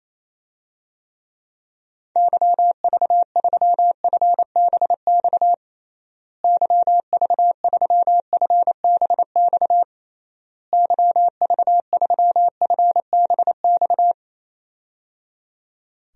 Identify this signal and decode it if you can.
Morse code, heard as a keyed tone, about 28 wpm, 700 Hz. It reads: YV3FBX YV3FBX YV3FBX
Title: DXCC-100 Call Signs - 6 Characters - Repeated 3x - 28wpm